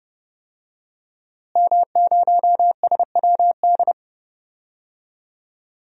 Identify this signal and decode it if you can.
Morse code, heard as a keyed tone, about 30 wpm, 700 Hz. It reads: M0SWD
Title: DXCC-100 Call Signs - 5 Characters - Repeated 2x - 30wpm